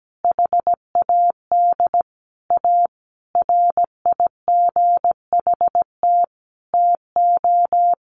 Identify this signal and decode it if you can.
Morse code, heard as a keyed tone, about 17 wpm, 700 Hz. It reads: HAD A RIGHT TO
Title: Sentences from Top 1300 Words 17wpm